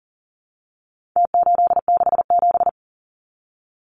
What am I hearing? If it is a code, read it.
Morse code, heard as a keyed tone, about 40 wpm, 700 Hz. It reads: T867
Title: Numbers - Contest Style - 4-Digit 40wpm